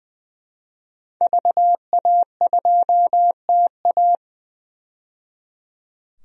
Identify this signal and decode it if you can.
Morse code, heard as a keyed tone, about 20 wpm, 700 Hz. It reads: VA2TA
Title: DXCC-100 Call Signs - 5 Characters - 20wpm